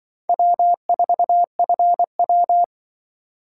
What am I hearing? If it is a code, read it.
Morse code, heard as a keyed tone, about 24 wpm, 690 Hz. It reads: W4FW
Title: Speed Racing - USA 1x2 Call Signs 20wpm